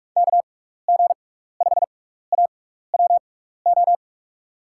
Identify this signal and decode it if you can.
Morse code, heard as a keyed tone, about 46 wpm, spaced as for 16 wpm, 700 Hz. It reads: KG5AWO
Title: USA Call Signs - 6 Characters - 6x Character Spacing - 45wpm